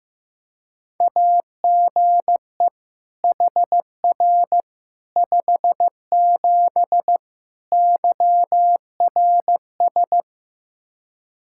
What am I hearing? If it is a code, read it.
Morse code, heard as a keyed tone, about 15 wpm, 700 Hz. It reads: AGE HR 57 YRS
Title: Practice QSO - Exchange 3 - More Info - 15wpm